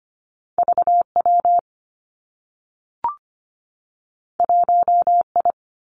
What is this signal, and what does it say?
Morse code, heard as a keyed tone, about 25 wpm, 700 Hz. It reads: VW 1I
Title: Two Challenging Letter-Number Combinations 25wpm